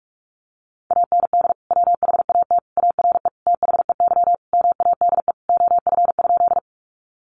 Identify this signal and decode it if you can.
Morse code, heard as a keyed tone, about 44 wpm, 700 Hz. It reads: AND WHAT ARE THEY MADE OF?
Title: Sentences from Top 200 Words 45wpm